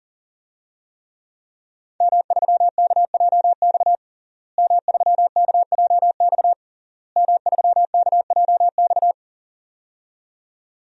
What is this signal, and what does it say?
Morse code, heard as a keyed tone, about 38 wpm, 700 Hz. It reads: M3KJX M3KJX M3KJX
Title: DXCC-100 Call Signs - 5 Characters - Repeated 3x - 40wpm